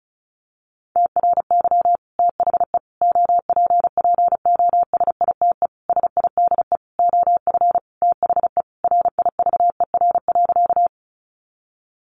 Morse code, heard as a keyed tone, about 34 wpm, 700 Hz. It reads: TRY THE OPPOSITE SIDE OF THE RIVER.
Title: Sentences from Top 1300 Words 35wpm